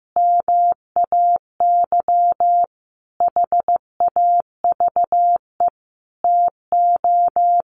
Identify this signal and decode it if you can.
Morse code, heard as a keyed tone, about 15 wpm, 700 Hz. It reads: MAY HAVE TO
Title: Sentences from Top 1100 Words 15wpm